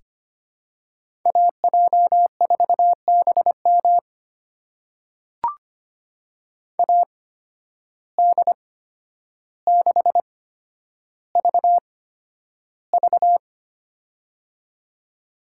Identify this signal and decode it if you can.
Morse code, heard as a keyed tone, about 25 wpm, spaced as for 11 wpm, 700 Hz. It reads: AJ4BM AD6VV
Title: USA Call Signs - 5 Characters - 8x Character Spacing - 25wpm